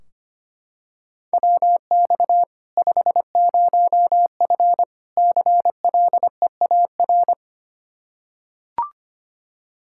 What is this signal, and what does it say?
Morse code, heard as a keyed tone, about 25 wpm, 700 Hz. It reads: WX 50F CLEAR
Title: Practice QSO - Exchange 2 - Info - 25wpm